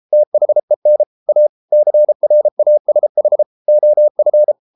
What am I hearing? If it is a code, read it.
Morse code, heard as a keyed tone, about 33 wpm, 600 Hz. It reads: THEN A CRASH OF